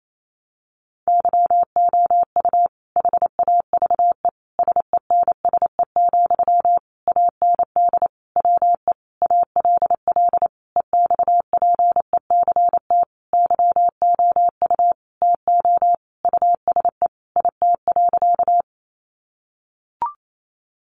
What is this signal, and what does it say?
Morse code, heard as a keyed tone, about 28 wpm, 700 Hz. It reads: YOU HAVE SENSE, AND WE ALL EXPECT YOU TO USE IT.
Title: Sentences from Top 1100 Words 28wpm